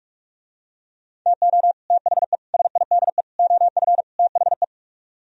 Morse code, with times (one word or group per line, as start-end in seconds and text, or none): 1.3-1.7 TO
1.9-2.4 THE
2.5-3.2 SIDE
3.4-4.0 OF
4.2-4.6 THE